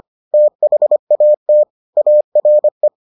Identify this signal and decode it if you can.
Morse code, heard as a keyed tone, about 25 wpm, 600 Hz. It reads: THAT ARE